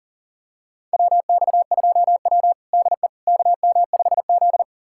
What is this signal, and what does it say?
Morse code, heard as a keyed tone, about 40 wpm, 700 Hz. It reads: WX2W DE KM5Z
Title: USA Callsign DE Callsign - 4 Characters - 40wpm